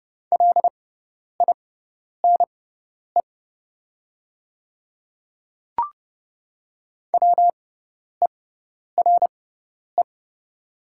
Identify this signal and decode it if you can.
Morse code, heard as a keyed tone, about 30 wpm, spaced as for 10 wpm, 700 Hz. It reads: LINE WERE
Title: 6x Character Spacing - Top 200 Words - 30wpm